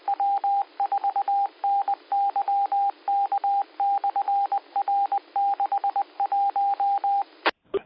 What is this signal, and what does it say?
Morse code, heard as a keyed tone, about 20 wpm, 800 Hz. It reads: W4NYK/R61